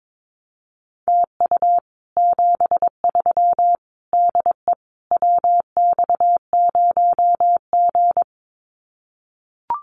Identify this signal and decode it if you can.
Morse code, heard as a keyed tone, about 22 wpm, 700 Hz. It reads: TU 73 DE WX0G
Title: Common QSO Phrases Encore 22wpm